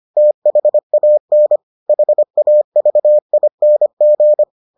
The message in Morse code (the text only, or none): THAN HAVING